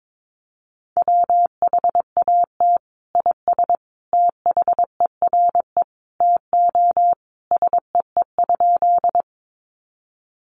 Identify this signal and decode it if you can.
Morse code, heard as a keyed tone, about 22 wpm, 700 Hz. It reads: WHAT IS THERE TO SEE?
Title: Sentences from Top 100 Words 22wpm